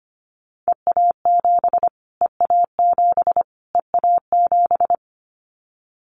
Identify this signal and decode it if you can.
Morse code, heard as a keyed tone, about 25 wpm, 700 Hz. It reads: EA7 EA7 EA7
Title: Callsign Prefixes for Top 25 DXCC - Repeated 3x - 25wpm